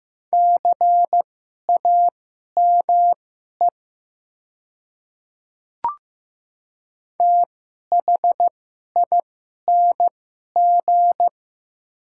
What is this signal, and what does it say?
Morse code, heard as a keyed tone, about 15 wpm, spaced as for 11 wpm, 700 Hz. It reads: CAME THING
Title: 2x Character Spacing - Top 200 Words - 15wpm